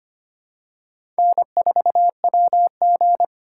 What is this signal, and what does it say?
Morse code, heard as a keyed tone, about 25 wpm, 700 Hz. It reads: N4WG
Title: USA Call Signs - 2x Character Spacing - 25wpm